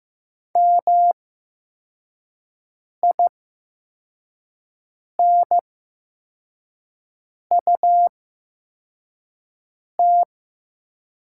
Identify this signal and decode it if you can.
Morse code, heard as a keyed tone, about 15 wpm, spaced as for 4 wpm, 700 Hz. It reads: MINUT
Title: 8x Character Spacing - Top 500 Words - 15wpm